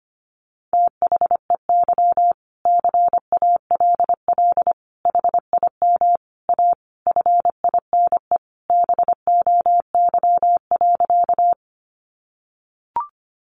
Morse code, heard as a keyed tone, about 25 wpm, 700 Hz. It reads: THEY CALL HIM A FINE BOY.
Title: Sentences from Top 500 Words 25wpm